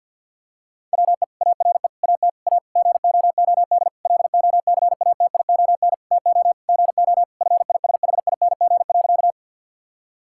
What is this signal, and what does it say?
Morse code, heard as a keyed tone, about 50 wpm, 700 Hz. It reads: WE ARE AT A GOOD LOCATION TO GO FISHING.
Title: Sentences from Top 1000 Words - Review Part 6 - 50wpm